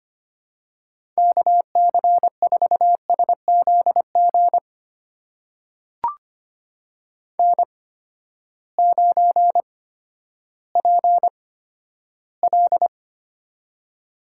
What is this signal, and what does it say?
Morse code, heard as a keyed tone, about 25 wpm, spaced as for 12 wpm, 700 Hz. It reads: KC4SZG N9PL
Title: USA Call Signs - 8x Character Spacing - 25wpm